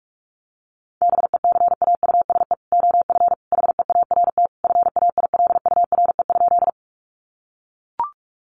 Morse code, heard as a keyed tone, about 44 wpm, 700 Hz. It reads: BECAUSE OF HEART FAILURE?
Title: Sentences from Top 1200 Words 45wpm